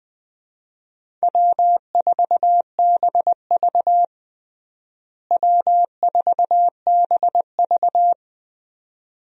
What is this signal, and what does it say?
Morse code, heard as a keyed tone, about 21 wpm, 700 Hz. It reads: W4BV W4BV
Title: USA Call Signs - 4 Characters - Repeated 2x - 20wpm